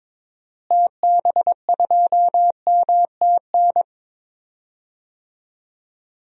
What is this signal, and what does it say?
Morse code, heard as a keyed tone, about 22 wpm, 700 Hz. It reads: TB2MTN
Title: DXCC-100 Call Signs - 6 Characters - Repeated 2x - 22wpm